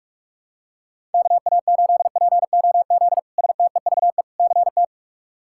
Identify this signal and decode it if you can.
Morse code, heard as a keyed tone, about 43 wpm, 700 Hz. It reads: KA8POZ STEVE CT
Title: K1USN SST QSO Practice - 45wpm